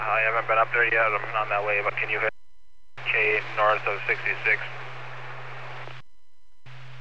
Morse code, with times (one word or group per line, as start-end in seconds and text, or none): none